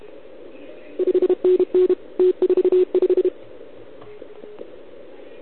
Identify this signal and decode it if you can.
Morse code, heard as a keyed tone, about 30 wpm, 410 Hz. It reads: 5NN T45
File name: ARCHIV / OKDX2020 / OL4Wa